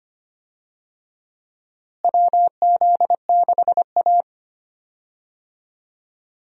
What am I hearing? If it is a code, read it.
Morse code, heard as a keyed tone, about 25 wpm, 700 Hz. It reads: WZ6A
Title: USA Call Signs - 4 Characters - Repeated 2x - 25wpm